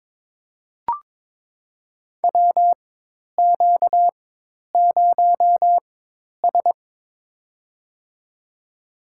Morse code, heard as a keyed tone, about 21 wpm, spaced as for 10 wpm, 700 Hz. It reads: WQ0S
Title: USA Call Signs - 4 Characters - 4x Character Spacing - 22wpm